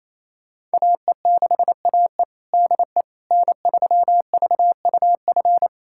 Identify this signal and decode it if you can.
Morse code, heard as a keyed tone, about 28 wpm, 700 Hz. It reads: AE6AE DE N3VUF
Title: USA Callsign DE Callsign - 5 Characters - 28wpm